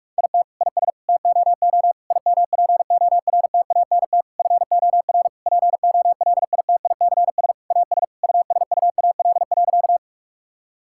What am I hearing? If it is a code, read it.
Morse code, heard as a keyed tone, about 45 wpm, 700 Hz. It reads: IT IS TOO IMPORTANT FOR POLITICS AS USUAL.
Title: Sentences from Top 1000 Words - Review Part 6 - 45wpm